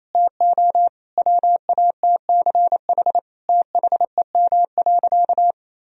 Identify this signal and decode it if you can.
Morse code, heard as a keyed tone, about 28 wpm, 700 Hz. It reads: TO WATCH THEM.
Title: Sentences from Top 1000 Words - Review Part 4 - 28wpm